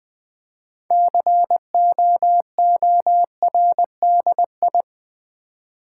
Morse code, heard as a keyed tone, about 20 wpm, 700 Hz. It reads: COORDI
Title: Build A Word - 6 Letters At A Time - 12-Letter Words - 20wpm